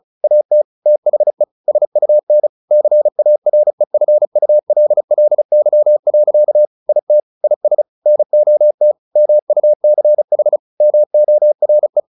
Morse code, heard as a keyed tone, about 34 wpm, 600 Hz. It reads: AT THE SUN CAREFULLY. IT IS NOT MUCH MORE
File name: the old man and the sea 2 35wpm